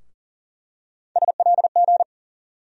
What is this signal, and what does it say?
Morse code, heard as a keyed tone, about 40 wpm, 700 Hz. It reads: SLG
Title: USA Call Sign Postfixes - 3 Letters 40wpm